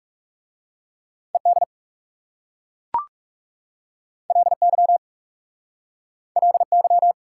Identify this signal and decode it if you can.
Morse code, heard as a keyed tone, about 38 wpm, 700 Hz. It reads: ED LY LY
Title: Speed-Racing Prefixes and Suffixes 30wpm